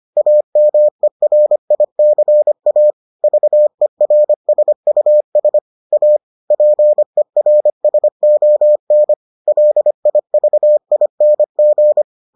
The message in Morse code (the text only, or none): AMERICA VERSUS A PERSON LIVING